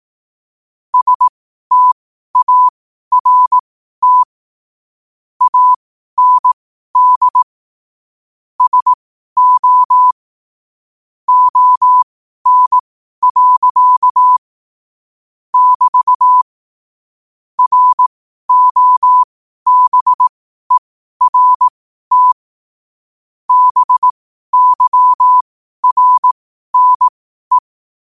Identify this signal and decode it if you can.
Morse code, heard as a keyed tone, about 18 wpm, spaced as for 12 wpm, 1000 Hz. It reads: START AND SO ON. <BT> ROBERT BYRNE
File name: 12wpm